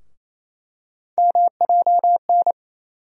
Morse code, heard as a keyed tone, about 27 wpm, 700 Hz. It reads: MJN
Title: USA Call Sign Postfixes - 3 Letters 28wpm